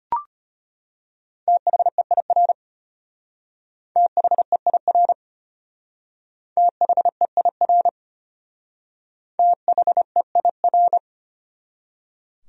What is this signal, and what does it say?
Morse code, heard as a keyed tone, about 30 wpm, 700 Hz. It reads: THEIR THEIR THEIR THEIR
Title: Speed-Racing Top 100 Words 25wpm